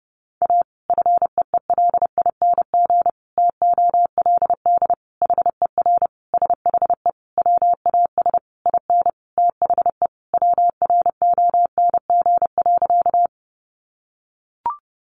Morse code, heard as a keyed tone, about 30 wpm, 700 Hz. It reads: A FEELING TOLD HER SHE WAS IN THE WRONG.
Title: Sentences from Top 1200 Words 30wpm